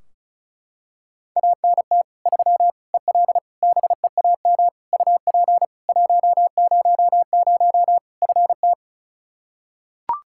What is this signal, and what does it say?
Morse code, heard as a keyed tone, about 34 wpm, 700 Hz. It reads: ANT 3 EL BEAM UP 100 FT
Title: Practice QSO - Exchange 2 - Info - 35wpm